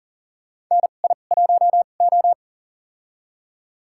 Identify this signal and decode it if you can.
Morse code, heard as a keyed tone, about 38 wpm, spaced as for 28 wpm, 700 Hz. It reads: NI1O
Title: USA Call Signs - 4 Characters - 2x Character Spacing - 40wpm